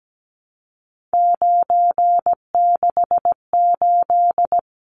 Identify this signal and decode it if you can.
Morse code, heard as a keyed tone, about 17 wpm, 700 Hz. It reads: 968
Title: Numbers - 3-Digit 17wpm